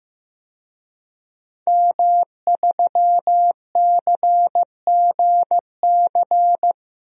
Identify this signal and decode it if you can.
Morse code, heard as a keyed tone, about 15 wpm, 700 Hz. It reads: M3CGC
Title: DXCC-100 Call Signs - 5 Characters - 15wpm